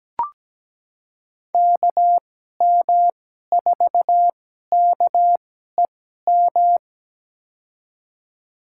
Morse code, heard as a keyed tone, about 17 wpm, spaced as for 12 wpm, 700 Hz. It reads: KM4KEM
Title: USA Call Signs - 6 Characters - 2x Character Spacing - 17wpm